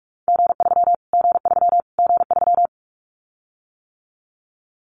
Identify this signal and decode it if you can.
Morse code, heard as a keyed tone, about 44 wpm, 700 Hz. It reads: G3 G3 G3
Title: Callsign Prefixes for Top 25 DXCC - Repeated 3x - 45wpm